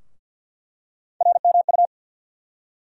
Morse code, heard as a keyed tone, about 50 wpm, 700 Hz. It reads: RMU
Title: USA Call Sign Postfixes - 3 Letters 50wpm